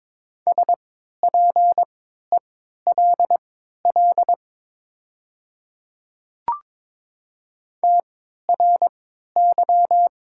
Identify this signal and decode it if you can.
Morse code, heard as a keyed tone, about 22 wpm, spaced as for 12 wpm, 700 Hz. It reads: SPELL TRY
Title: 3x Character Spacing - Top 500 Words - 22wpm